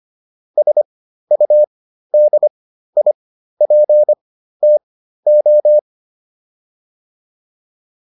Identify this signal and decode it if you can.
Morse code, heard as a keyed tone, about 25 wpm, spaced as for 13 wpm, 600 Hz. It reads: SUDIPTO